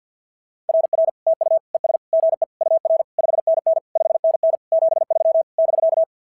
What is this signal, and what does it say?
Morse code, heard as a keyed tone, about 54 wpm, 650 Hz. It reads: RR TU ES GE UR 5NN 5NN 73 <BK>
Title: SOTA QSO Practice 50wpm